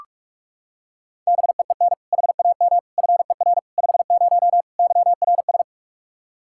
Morse code, heard as a keyed tone, about 46 wpm, 700 Hz. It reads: BEEN HAM FER 50 YRS
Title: K1USN SST QSO Practice 45wpm